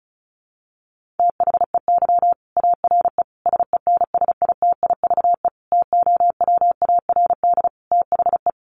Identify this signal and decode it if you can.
Morse code, heard as a keyed tone, about 34 wpm, 700 Hz. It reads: THEY ARE SENSITIVE TOWARD THE
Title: Sentences from Top 1400 Words 35wpm